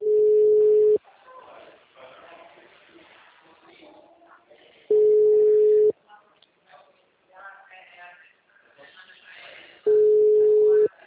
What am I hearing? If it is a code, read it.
Ringback tone (Europe, 425 Hz), with speech.
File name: out- -2022- - - .31720